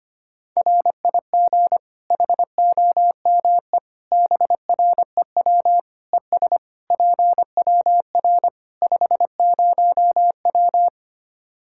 Morse code, heard as a keyed tone, about 25 wpm, 700 Hz. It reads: RIG HOME BREW ES PWR 50W
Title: Practice QSO - Exchange 2 - Info - 25wpm